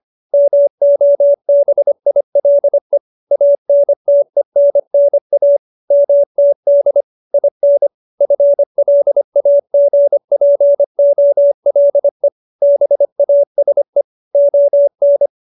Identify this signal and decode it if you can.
Morse code, heard as a keyed tone, about 25 wpm, 570 Hz. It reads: MOBILE ANTENNA MTD IN FLAGPOLE BASE ON